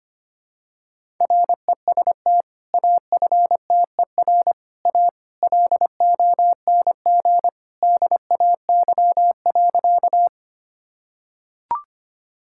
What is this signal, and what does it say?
Morse code, heard as a keyed tone, about 25 wpm, 700 Hz. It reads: REST AFTER A LONG DAY.
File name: practice / 25wpm / Sentences with 2-Word Phrases - 25wpm